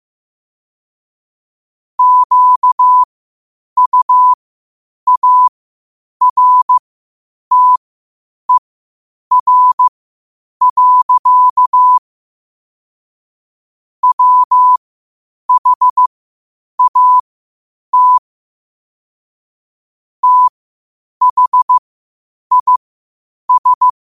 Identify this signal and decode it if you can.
Morse code, heard as a keyed tone, about 15 wpm, spaced as for 8 wpm, 1000 Hz. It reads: QUARTER. WHAT THIS
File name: qotdfiles / 8wpm